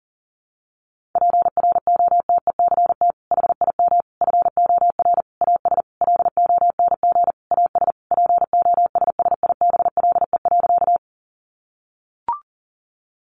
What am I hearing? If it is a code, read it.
Morse code, heard as a keyed tone, about 40 wpm, 700 Hz. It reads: PROTECT HIM FOR AS LONG AS POSSIBLE.
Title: Sentences from Top 900 Words 40wpm